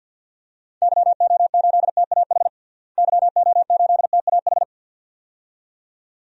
Morse code, heard as a keyed tone, about 48 wpm, 700 Hz. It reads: YO8TAH YO8TAH
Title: DXCC-100 Call Signs - 6 Characters - Repeated 2x - 50wpm